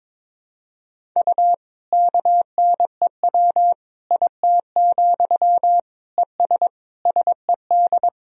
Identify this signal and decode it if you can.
Morse code, heard as a keyed tone, about 21 wpm, 700 Hz. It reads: U KNEW IT, ES SED
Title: Sentences from Top 500 Words and Abbreviations 22wpm